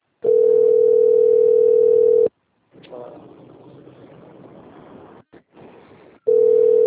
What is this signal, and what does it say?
Ringback tone (North America), 440 + 480 Hz.
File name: rg-1600-unknown- - - .7966